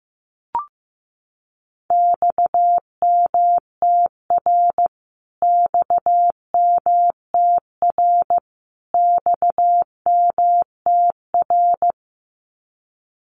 Morse code, heard as a keyed tone, about 15 wpm, 700 Hz. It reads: XMTR XMTR XMTR
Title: Abbreviations Repeated 3x - 15wpm